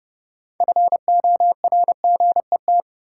Morse code, heard as a keyed tone, about 30 wpm, 700 Hz. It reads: FORGET
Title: Sentences from Top 1200 Words - New Word Drill Alt - 30wpm